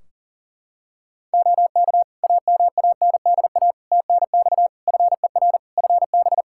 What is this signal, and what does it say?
Morse code, heard as a keyed tone, about 40 wpm, 700 Hz. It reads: OK AMANDA TNX FER FB
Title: K1USN SST QSO Practice 40wpm